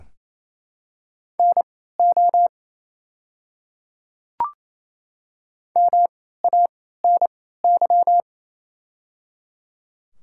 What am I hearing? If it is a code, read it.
Morse code, heard as a keyed tone, about 28 wpm, spaced as for 16 wpm, 700 Hz. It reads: NO MANY
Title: 3x Character Spacing - Top 100 Words - 28wpm